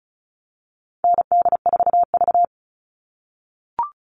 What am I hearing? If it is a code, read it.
Morse code, heard as a keyed tone, about 34 wpm, 700 Hz. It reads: ND4V
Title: All Cut Numbers - 4-Digit 35wpm